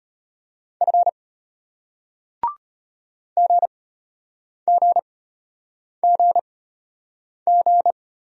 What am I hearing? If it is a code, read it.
Morse code, heard as a keyed tone, about 33 wpm, spaced as for 8 wpm, 700 Hz. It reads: F GGGG